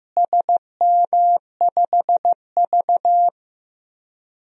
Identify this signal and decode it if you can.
Morse code, heard as a keyed tone, about 15 wpm, 700 Hz. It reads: SM5V
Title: DXCC-100 Call Signs - 4 Characters - Repeated 2x - 15wpm